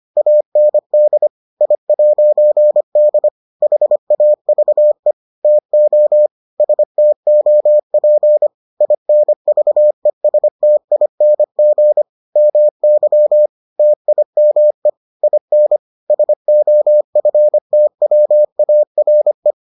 Morse code, heard as a keyed tone, about 25 wpm, 600 Hz. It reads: AND I'D HAVE TO STOP INVESTING MY TIME IN SOFTWARE